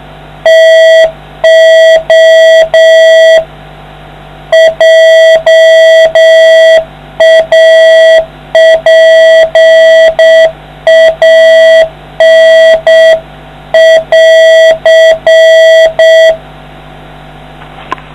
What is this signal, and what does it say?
Morse code, heard as a keyed tone, about 6 wpm, 650 Hz, with noise. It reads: TO JAPAN <AR>